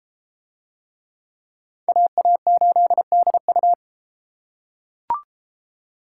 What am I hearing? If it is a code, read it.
Morse code, heard as a keyed tone, about 31 wpm, 700 Hz. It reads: AA8DU